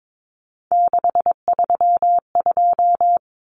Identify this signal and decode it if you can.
Morse code, heard as a keyed tone, about 22 wpm, 700 Hz. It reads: 632
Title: Numbers - 3-Digit 22wpm